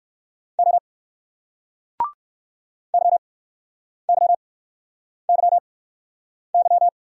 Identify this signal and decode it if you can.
Morse code, heard as a keyed tone, about 60 wpm, spaced as for 9 wpm, 700 Hz. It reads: O YYYY